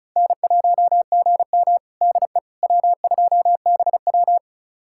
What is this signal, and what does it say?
Morse code, heard as a keyed tone, about 36 wpm, 700 Hz. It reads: N1GM DE W2BW
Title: USA Callsign DE Callsign - 4 Characters - 35wpm